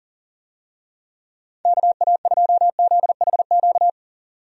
Morse code, heard as a keyed tone, about 37 wpm, 700 Hz. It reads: KA2ZHQ
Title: USA Call Signs - 6 Characters - 40wpm